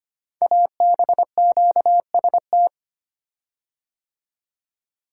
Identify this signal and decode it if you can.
Morse code, heard as a keyed tone, about 25 wpm, 700 Hz. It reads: ABQST
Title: Code Groups - 5 Letters - 25wpm